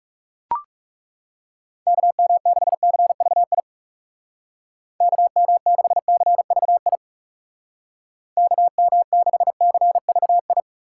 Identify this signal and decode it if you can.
Morse code, heard as a keyed tone, about 40 wpm, 700 Hz. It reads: KM6CVI KM6CVI KM6CVI